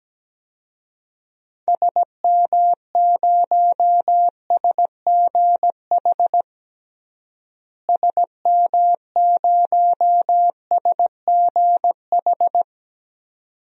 Morse code, heard as a keyed tone, about 17 wpm, 700 Hz. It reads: SM0SGH SM0SGH
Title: DXCC-100 Call Signs - 6 Characters - Repeated 2x - 17wpm